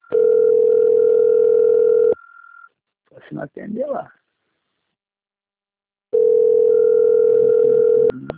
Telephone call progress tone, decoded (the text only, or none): ringback tone